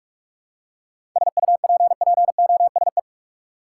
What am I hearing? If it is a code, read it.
Morse code, heard as a keyed tone, about 46 wpm, 700 Hz. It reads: SUPPOSE
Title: Seven Letter Words - 45wpm